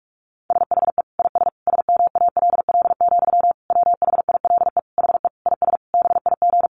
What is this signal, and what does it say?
Morse code, heard as a keyed tone, about 44 wpm, 700 Hz. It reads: SHE IS SMALL, WHILE HE IS BIG
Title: Sentences from Top 1300 Words 45wpm